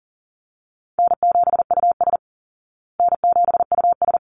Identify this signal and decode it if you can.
Morse code, heard as a keyed tone, about 40 wpm, 700 Hz. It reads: N7US N7US
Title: Call Signs Repeated 3x - 40wpm